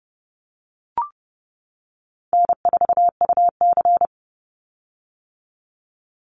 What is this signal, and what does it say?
Morse code, heard as a keyed tone, about 30 wpm, 700 Hz. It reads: N4UC
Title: Call Signs - 30wpm